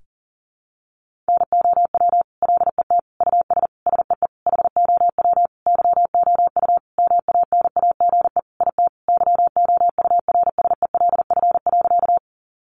Morse code, heard as a keyed tone, about 40 wpm, 700 Hz. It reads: NOW LET US SEE HOW YOU MANAGE IT YOURSELF.
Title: Sentences from Top 1400 Words 40wpm